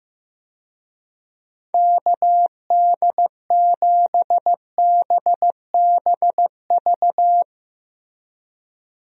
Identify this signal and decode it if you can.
Morse code, heard as a keyed tone, about 16 wpm, 700 Hz. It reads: KD7BBV
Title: USA Call Signs - 6 Characters - Repeated 2x - 15wpm